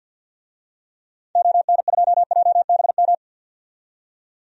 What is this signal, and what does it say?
Morse code, heard as a keyed tone, about 50 wpm, 700 Hz. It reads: ON2JBM
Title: DXCC-100 Call Signs - 6 Characters - 50wpm